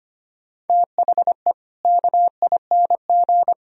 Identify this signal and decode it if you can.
Morse code, heard as a keyed tone, about 25 wpm, 700 Hz. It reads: THE KING
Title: Sentences from Top 1000 Words - Review Part 4 - 25wpm